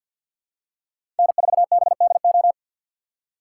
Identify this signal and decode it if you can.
Morse code, heard as a keyed tone, about 47 wpm, 700 Hz. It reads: N4BDO
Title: USA Call Signs - 5 Characters - 6x Character Spacing - 50wpm